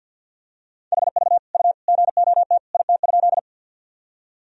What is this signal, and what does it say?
Morse code, heard as a keyed tone, about 46 wpm, 700 Hz. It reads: HV U GOT IT?
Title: Sentences from Top 300 Words and Abbreviations 50wpm